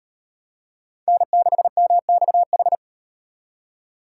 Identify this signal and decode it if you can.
Morse code, heard as a keyed tone, about 38 wpm, 700 Hz. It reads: N6MXH